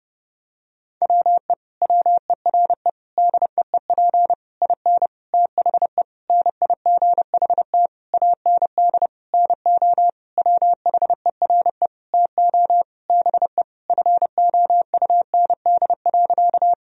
Morse code, heard as a keyed tone, about 30 wpm, 700 Hz. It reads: WE WERE DEEP IN THE NIGHT AND NO WHERE TO BE FOUND.
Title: Sentences from Top 1000 Words - Review Part 3 - 30wpm